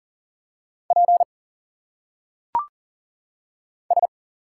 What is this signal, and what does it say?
Morse code, heard as a keyed tone, about 40 wpm, 700 Hz. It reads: P S
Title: Single Letters - Phonetic Pronunciation 40wpm